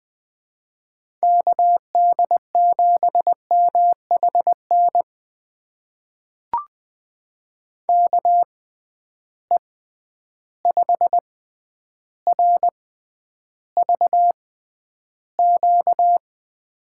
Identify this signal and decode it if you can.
Morse code, heard as a keyed tone, about 20 wpm, spaced as for 10 wpm, 700 Hz. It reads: KD7MHN KE5RVQ
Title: USA Call Signs - 6 Characters - 6x Character Spacing - 20wpm